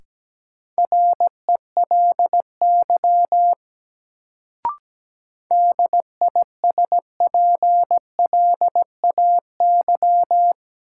Morse code, heard as a keyed tone, about 17 wpm, 700 Hz. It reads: RELY DISPLAY
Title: Sentences from Top 1200 Words - New Word Drill Alt - 17wpm